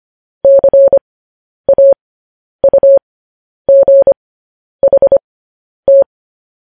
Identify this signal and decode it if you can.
Morse code, heard as a keyed tone, about 25 wpm, spaced as for 10 wpm, 550 Hz. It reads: CAUGHT
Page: news-headlines-in-morse-code-at-10-wpm